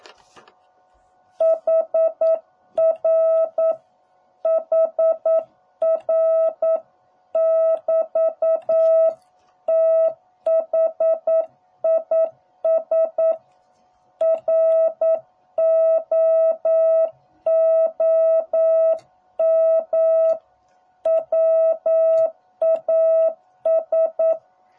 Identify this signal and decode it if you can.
Morse code, heard as a keyed tone, about 9 wpm, 650 Hz. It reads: HR HR <BT> THIS ROOM WAS